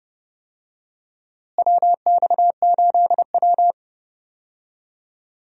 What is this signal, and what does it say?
Morse code, heard as a keyed tone, about 30 wpm, 700 Hz. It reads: WX8W
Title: USA Call Signs - 4 Characters - Repeated 3x - 30wpm